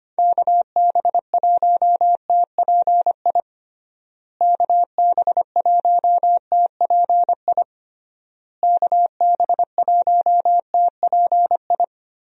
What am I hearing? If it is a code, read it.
Morse code, heard as a keyed tone, about 25 wpm, 700 Hz. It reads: KB1TPI KB1TPI KB1TPI
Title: USA Call Signs - 6 Characters - Repeated 3x - 25wpm